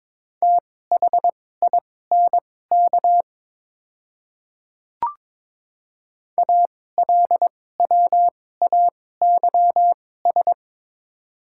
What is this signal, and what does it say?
Morse code, heard as a keyed tone, about 21 wpm, spaced as for 16 wpm, 700 Hz. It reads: THINK ALWAYS
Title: 2x Character Spacing - Top 300 Words - 22wpm